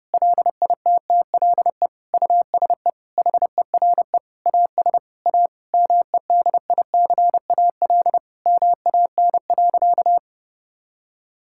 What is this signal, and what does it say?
Morse code, heard as a keyed tone, about 30 wpm, 700 Hz. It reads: LITTLE USE HERE AS A MEDICAL MAN.
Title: Sentences from Top 1000 Words - Review Part 5 - 30wpm